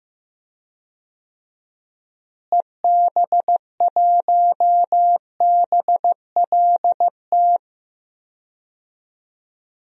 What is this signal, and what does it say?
Morse code, heard as a keyed tone, about 15 wpm, 700 Hz. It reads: EB1BLT
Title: DXCC-100 Call Signs - 6 Characters - Repeated 3x - 15wpm